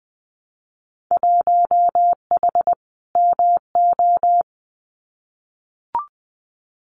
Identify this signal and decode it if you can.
Morse code, heard as a keyed tone, about 20 wpm, 700 Hz. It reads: 1H MO
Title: Winter Field Day Exchanges 20wpm